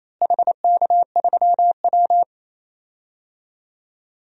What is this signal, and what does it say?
Morse code, heard as a keyed tone, about 28 wpm, 700 Hz. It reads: HK3W
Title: DXCC-100 Call Signs - 4 Characters - Repeated 3x - 28wpm